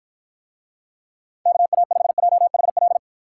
Morse code, heard as a keyed tone, about 54 wpm, 700 Hz. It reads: KA5JHL